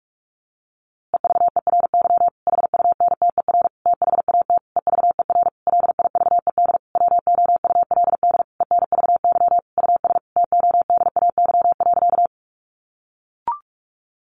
Morse code, heard as a keyed tone, about 44 wpm, 700 Hz. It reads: EVERY HUNTER THAT EVER LIVED WOULD ENVY US TODAY.
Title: A Sound Of Thunder 45wpm